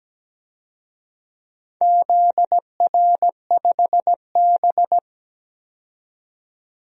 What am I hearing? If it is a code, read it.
Morse code, heard as a keyed tone, about 17 wpm, 700 Hz. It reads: ZR5B
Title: DXCC-100 Call Signs - 4 Characters - 17wpm